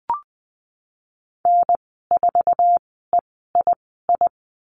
Morse code, heard as a keyed tone, about 20 wpm, spaced as for 15 wpm, 700 Hz. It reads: N4EII
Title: 055 Lesson 10g - Letter r - Call Signs 20wpm